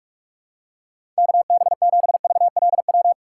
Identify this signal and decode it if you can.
Morse code, heard as a keyed tone, about 42 wpm, 700 Hz. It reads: KB7VLW
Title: USA Call Signs - 6 Characters - 6x Character Spacing - 45wpm